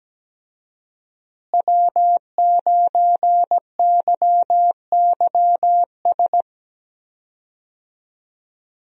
Morse code, heard as a keyed tone, about 17 wpm, 700 Hz. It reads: W9YYS
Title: USA Call Signs - 5 Characters - Repeated 2x - 17wpm